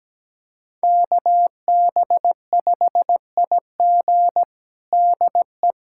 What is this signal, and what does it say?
Morse code, heard as a keyed tone, about 17 wpm, 700 Hz. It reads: KB5IG DE
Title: USA Callsign DE Callsign - 5 Characters - 17wpm